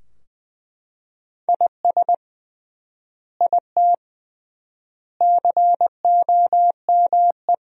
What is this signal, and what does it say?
Morse code, heard as a keyed tone, about 20 wpm, 700 Hz. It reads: IS IT COME
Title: 3x Word-Spacing - Sentences from Top 500 Words - 20wpm